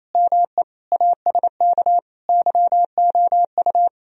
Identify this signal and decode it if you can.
Morse code, heard as a keyed tone, about 35 wpm, 700 Hz. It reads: ME ASK YOU
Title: Speed-Racing - Sets of 5 Words 28wpm